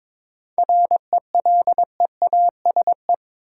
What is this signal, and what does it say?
Morse code, heard as a keyed tone, about 21 wpm, 700 Hz. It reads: RELEASE
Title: Sentences from Top 1100 Words - New Word Drill - 22wpm